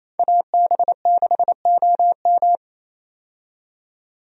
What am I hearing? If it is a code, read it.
Morse code, heard as a keyed tone, about 28 wpm, 700 Hz. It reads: AB6OM
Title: USA Call Signs - 5 Characters - 28wpm